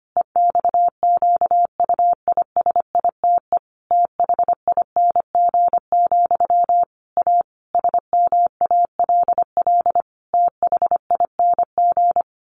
Morse code, heard as a keyed tone, about 25 wpm, 700 Hz. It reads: EXQUISITE THING, A SMALL THING
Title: A Sound Of Thunder 25wpm